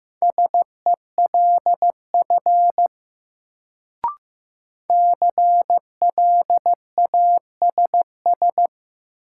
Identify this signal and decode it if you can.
Morse code, heard as a keyed tone, about 15 wpm, 700 Hz. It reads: SELF CLASS
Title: Sentences from Top 1400 Words - New Word Drill Alt - 15wpm